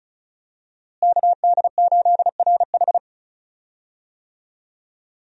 Morse code, heard as a keyed tone, about 36 wpm, 700 Hz. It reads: KD8RH
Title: USA Call Signs - 5 Characters - Repeated 2x - 35wpm